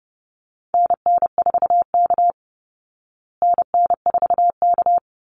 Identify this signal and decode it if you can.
Morse code, heard as a keyed tone, about 30 wpm, 700 Hz. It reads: NN4K NN4K
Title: Call Signs Repeated 3x - 30wpm